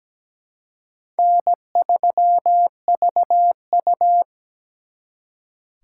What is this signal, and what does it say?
Morse code, heard as a keyed tone, about 20 wpm, 700 Hz. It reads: N3VU
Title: Speed Racing - USA 1x2 Call Signs 17wpm